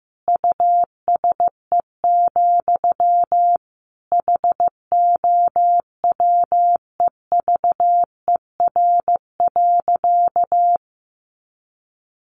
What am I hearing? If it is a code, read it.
Morse code, heard as a keyed tone, about 15 wpm, 700 Hz. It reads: USE, HOWEVER.
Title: Sentences from Top 1400 Words 15wpm